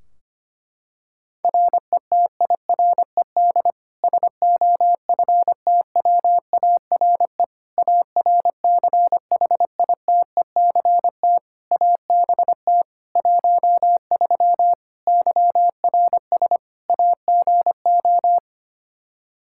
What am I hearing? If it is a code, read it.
Morse code, heard as a keyed tone, about 25 wpm, 700 Hz. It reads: RETIRED SOFTWARE ARCHITECT ABT 13 YRS AGO
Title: Practice QSO - Exchange 3 - More Info - 25wpm